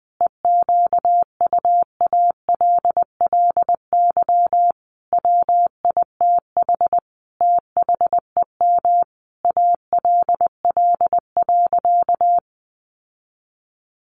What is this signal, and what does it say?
Morse code, heard as a keyed tone, about 20 wpm, 700 Hz. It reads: EQUALLY WITH THEM ALL.
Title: Sentences from Top 1400 Words 20wpm